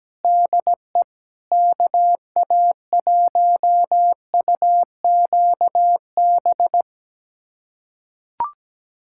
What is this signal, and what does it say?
Morse code, heard as a keyed tone, about 17 wpm, 700 Hz. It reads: DE KA1UQB
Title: USA Callsign DE Callsign - 6 Characters - 17wpm